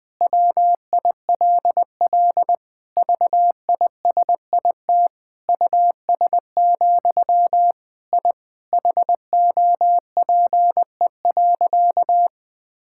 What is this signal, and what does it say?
Morse code, heard as a keyed tone, about 20 wpm, 700 Hz. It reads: WILL VISIT US, I HOPE.
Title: Sentences from Top 1000 Words - Review Part 6 - 20wpm